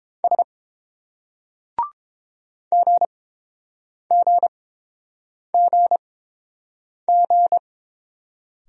Morse code, heard as a keyed tone, about 30 wpm, spaced as for 7 wpm, 700 Hz. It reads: S GGGG